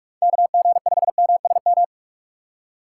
Morse code, heard as a keyed tone, about 46 wpm, 700 Hz. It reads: KG5MSM
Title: USA Call Signs - 6 Characters - 4x Character Spacing - 45wpm